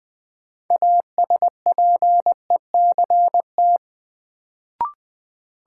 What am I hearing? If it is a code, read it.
Morse code, heard as a keyed tone, about 21 wpm, 700 Hz. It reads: ASPECT
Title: Sentences from Top 1300 Words - New Word Drill Alt - 20wpm